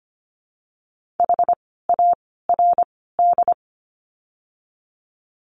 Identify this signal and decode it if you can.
Morse code, heard as a keyed tone, about 25 wpm, 700 Hz. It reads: H A R D
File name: course / 25:10wpm / 094 Lesson 16f - Letter c - Words 25wpm